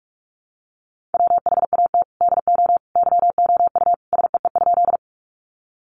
Morse code, heard as a keyed tone, about 44 wpm, 700 Hz. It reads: WHAT DO YOU SEE?
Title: Sentences from Top 100 Words 45wpm